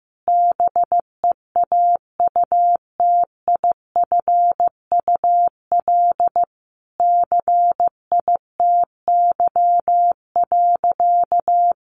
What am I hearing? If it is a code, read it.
Morse code, heard as a keyed tone, about 15 wpm, 700 Hz. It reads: BEAUTIFUL CITY.
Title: Sentences from Top 1100 Words 15wpm